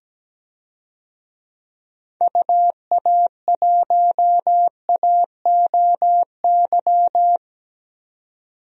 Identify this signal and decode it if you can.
Morse code, heard as a keyed tone, about 17 wpm, 700 Hz. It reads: UA1AOY
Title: DXCC-100 Call Signs - 6 Characters - 17wpm